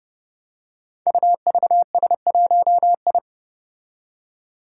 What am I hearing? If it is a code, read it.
Morse code, heard as a keyed tone, about 29 wpm, 700 Hz. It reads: UVS1I
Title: Five Challenging Letter-Number Combinations 30wpm